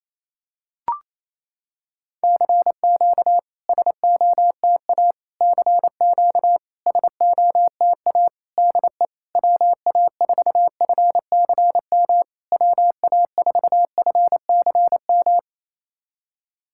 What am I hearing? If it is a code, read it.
Morse code, heard as a keyed tone, about 27 wpm, 690 Hz. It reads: CQ SOTA CQ SOTA DE WA4FCM WA4FCM
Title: SOTA QSO Practice 28wpm